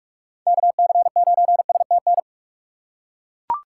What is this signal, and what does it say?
Morse code, heard as a keyed tone, about 42 wpm, 700 Hz. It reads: KC9STN
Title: USA Call Signs - 6 Characters - 6x Character Spacing - 45wpm